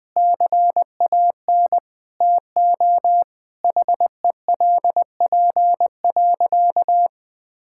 Morse code, heard as a keyed tone, about 20 wpm, 700 Hz. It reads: CAN TO HELP.
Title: Sentences with 8-Word Phrases - Spoken Beginning - 20wpm